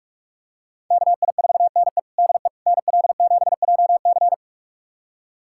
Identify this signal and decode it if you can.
Morse code, heard as a keyed tone, about 46 wpm, 700 Hz. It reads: KI4NE DE NL7JC
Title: USA Callsign DE Callsign 45wpm